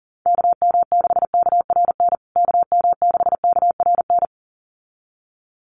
Morse code, heard as a keyed tone, about 40 wpm, 700 Hz. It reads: KM6KRN KM6KRN
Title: USA Call Signs Repeated 2x 40wpm